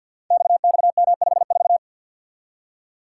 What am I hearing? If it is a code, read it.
Morse code, heard as a keyed tone, about 51 wpm, 700 Hz. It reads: XXM54
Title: Code Groups - 5 Letter:Numbers - 50wpm